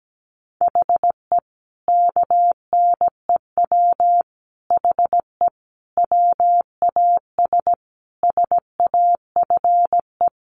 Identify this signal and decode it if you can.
Morse code, heard as a keyed tone, about 17 wpm, 700 Hz. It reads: HE KNEW HE WAS SAFE
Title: Sentences from Top 900 Words 17wpm